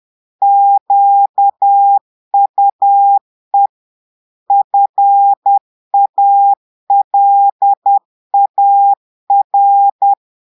Morse code, heard as a keyed tone, about 10 wpm, 800 Hz. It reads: QUE FALAR
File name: portugais texte en clair 2 vitesse 10 mots minutes